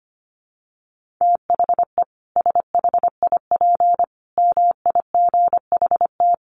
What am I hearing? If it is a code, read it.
Morse code, heard as a keyed tone, about 25 wpm, 700 Hz. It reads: THE SHIP MIGHT
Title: Sentences from Top 1400 Words 25wpm